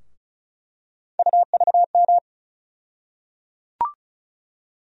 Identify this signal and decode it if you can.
Morse code, heard as a keyed tone, about 36 wpm, 700 Hz. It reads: UVM
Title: USA Call Sign Postfixes - 3 Letters 35wpm